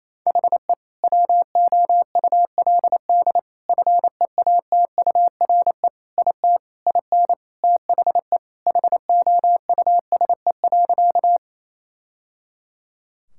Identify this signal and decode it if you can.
Morse code, heard as a keyed tone, about 28 wpm, 700 Hz. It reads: HE WOULD FEATURE IT IN THE HOUSE.
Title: Sentences from Top 1000 Words - Review Part 5 - 28wpm